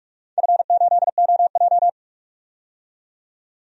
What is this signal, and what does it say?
Morse code, heard as a keyed tone, about 44 wpm, 700 Hz. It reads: F8OJ
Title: DXCC-100 Call Signs - 4 Characters - 45wpm